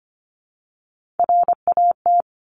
Morse code, heard as a keyed tone, about 25 wpm, 700 Hz. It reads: RAT
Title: Top Three Letter Combinations in Words 25wpm